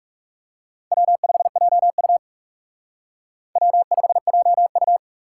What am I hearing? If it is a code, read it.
Morse code, heard as a keyed tone, about 37 wpm, 700 Hz. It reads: W5JU W5JU
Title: Speed Racing - USA 1x2 Call Signs 30wpm